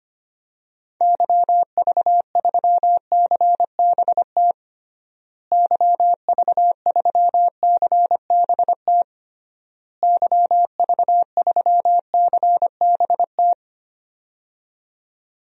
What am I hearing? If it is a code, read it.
Morse code, heard as a keyed tone, about 25 wpm, 700 Hz. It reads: YV3CBT YV3CBT YV3CBT
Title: DXCC-100 Call Signs - 6 Characters - Repeated 3x - 25wpm